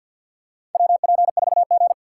Morse code, heard as a keyed tone, about 53 wpm, 700 Hz. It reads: WP4G
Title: USA Call Signs - 4x Character Spacing - 50wpm